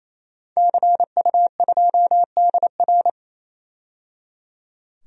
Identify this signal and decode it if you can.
Morse code, heard as a keyed tone, about 28 wpm, 700 Hz. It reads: CU2DR
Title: DXCC-100 Call Signs - 5 Characters - 28wpm